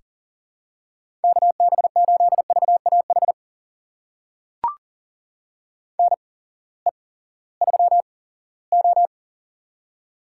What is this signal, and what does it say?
Morse code, heard as a keyed tone, about 40 wpm, spaced as for 20 wpm, 700 Hz. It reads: KB8VAH NE3O
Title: USA Call Signs - 8x Character Spacing - 40wpm